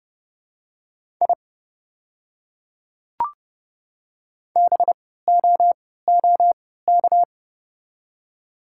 Morse code, heard as a keyed tone, about 30 wpm, spaced as for 17 wpm, 700 Hz. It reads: I BOOK
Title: 3x Character Spacing - Top 300 Words - 30wpm